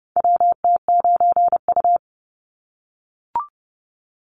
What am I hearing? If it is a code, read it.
Morse code, heard as a keyed tone, about 30 wpm, 700 Hz. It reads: WT9U
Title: Call Signs - 30wpm